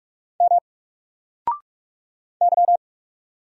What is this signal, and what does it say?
Morse code, heard as a keyed tone, about 46 wpm, 700 Hz. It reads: M Y
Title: Single Letter-Number 45wpm